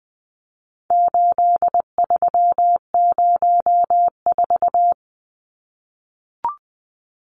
Morse code, heard as a keyed tone, about 20 wpm, 700 Hz. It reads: 8304
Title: Numbers - 4-Digit 20wpm